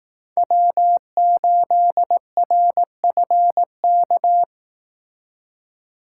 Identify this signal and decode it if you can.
Morse code, heard as a keyed tone, about 18 wpm, 700 Hz. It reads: W8RFK
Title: Speed Racing - USA 1x3 Call Signs 15wpm